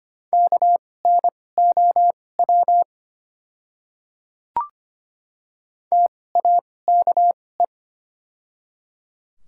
Morse code, heard as a keyed tone, about 25 wpm, spaced as for 18 wpm, 700 Hz. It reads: KNOW TAKE
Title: 2x Character Spacing - Top 500 Words - 25wpm